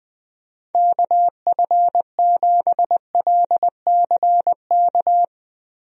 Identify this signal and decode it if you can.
Morse code, heard as a keyed tone, about 20 wpm, 700 Hz. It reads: KF7LCK
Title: USA Call Signs - 6 Characters - Repeated 3x - 20wpm